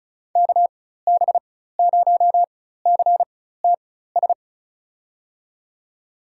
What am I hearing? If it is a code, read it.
Morse code, heard as a keyed tone, about 36 wpm, spaced as for 16 wpm, 700 Hz. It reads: KB0CTS
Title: USA Call Signs - 6 Characters - 4x Character Spacing - 35wpm